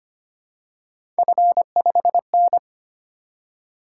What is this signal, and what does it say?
Morse code, heard as a keyed tone, about 25 wpm, 700 Hz. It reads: F5N
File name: Three Letter-Number Combinations - 25wpm